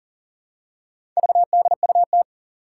Morse code, heard as a keyed tone, about 40 wpm, 700 Hz. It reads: VDUT
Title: Four Challenging Letter-Number Combinations 40wpm